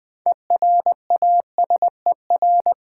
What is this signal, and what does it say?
Morse code, heard as a keyed tone, about 20 wpm, 700 Hz. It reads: ERASER
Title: Easy Words with Hints - 20wpm